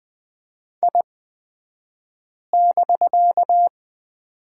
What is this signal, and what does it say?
Morse code, heard as a keyed tone, about 20 wpm, spaced as for 5 wpm, 700 Hz. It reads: I<BK>
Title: 249 Lesson 40d - BK - Two Characters 20wpm